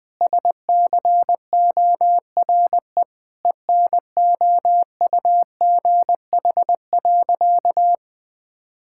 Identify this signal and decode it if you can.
Morse code, heard as a keyed tone, about 20 wpm, 700 Hz. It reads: SCORE ENOUGH.
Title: Sentences from Top 1600 Words 20wpm